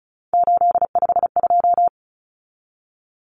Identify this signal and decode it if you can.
Morse code, heard as a keyed tone, about 34 wpm, 700 Hz. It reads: 852
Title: Common Cut Numbers - 3-Digit 35wpm